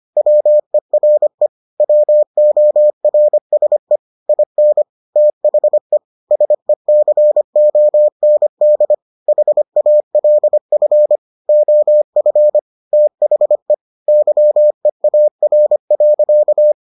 Morse code, heard as a keyed tone, about 25 wpm, 600 Hz. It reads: WERE WORSE IN THE SECOND HALF OF THE YEAR.